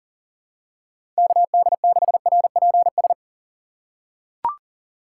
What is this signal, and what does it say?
Morse code, heard as a keyed tone, about 40 wpm, 700 Hz. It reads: KD6RPS
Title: USA Call Signs - 6 Characters - 3x Character Spacing - 40wpm